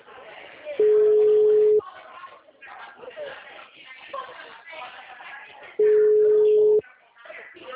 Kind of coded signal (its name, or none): telephone call progress tone